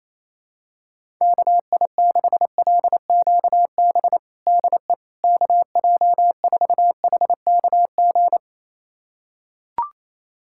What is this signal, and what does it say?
Morse code, heard as a keyed tone, about 28 wpm, 700 Hz. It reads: KI6LQB DE KJ4HKG
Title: USA Callsign DE Callsign - 6 Characters - 28wpm